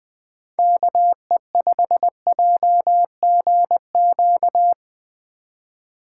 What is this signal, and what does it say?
Morse code, heard as a keyed tone, about 21 wpm, 700 Hz. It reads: KE5JGQ